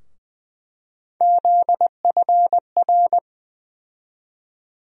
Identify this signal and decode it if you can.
Morse code, heard as a keyed tone, about 20 wpm, 700 Hz. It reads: ZFR
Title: USA Call Sign Postfixes - 3 Letters 20wpm